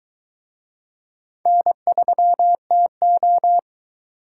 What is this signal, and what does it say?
Morse code, heard as a keyed tone, about 21 wpm, 700 Hz. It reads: N3TO